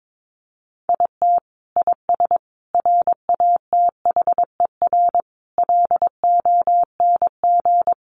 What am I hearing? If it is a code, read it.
Morse code, heard as a keyed tone, about 22 wpm, 700 Hz. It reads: IT IS RATHER LONG
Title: Sentences from Top 700 Words 22wpm